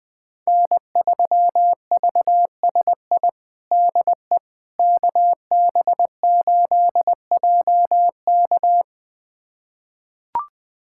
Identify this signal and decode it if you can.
Morse code, heard as a keyed tone, about 20 wpm, 700 Hz. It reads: N3VSI DE KB8JK
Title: USA Callsign DE Callsign - 5 Characters - 20wpm